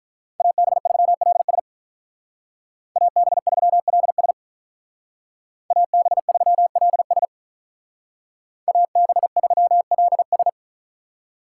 Morse code, heard as a keyed tone, about 45 wpm, 700 Hz. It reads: AB3LS AB3LS AB3LS AB3LS